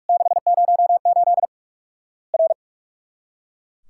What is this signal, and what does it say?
Morse code, heard as a keyed tone, about 44 wpm, 700 Hz. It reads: 608 R
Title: Worked All Europe DX Contest - QTC Exchange - 45wpm